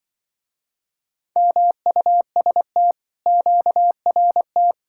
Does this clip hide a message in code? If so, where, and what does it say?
Yes: Morse code from 1.4 to 4.7 s, heard as a keyed tone, about 25 wpm, 700 Hz. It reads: MUST QRT